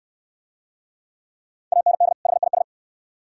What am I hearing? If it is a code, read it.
Morse code, heard as a keyed tone, about 60 wpm, 700 Hz. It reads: AND HIS